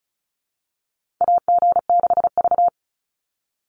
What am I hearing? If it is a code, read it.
Morse code, heard as a keyed tone, about 34 wpm, 700 Hz. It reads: AG6V
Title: NA SOTA Chaser Callsigns 35wpm